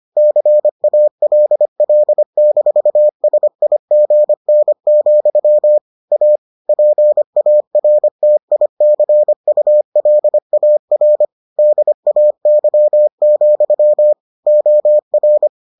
Morse code, heard as a keyed tone, about 25 wpm, 600 Hz. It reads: CALL-SIGN, A PARTICULAR DAY, OR